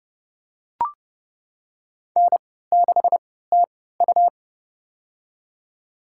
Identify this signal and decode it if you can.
Morse code, heard as a keyed tone, about 30 wpm, spaced as for 17 wpm, 700 Hz. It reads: N6TU
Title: USA Call Signs - 3x Character Spacing - 30wpm